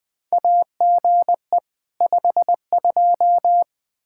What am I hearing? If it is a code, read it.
Morse code, heard as a keyed tone, about 20 wpm, 700 Hz. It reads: AGE 52
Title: Practice QSO - Exchange 3 - More Info - 20wpm